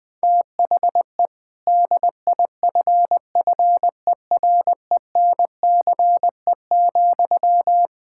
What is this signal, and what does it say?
Morse code, heard as a keyed tone, about 20 wpm, 700 Hz. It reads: THE DIFFERENCE,
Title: Sentences from Top 1000 Words - Review Part 5 - 20wpm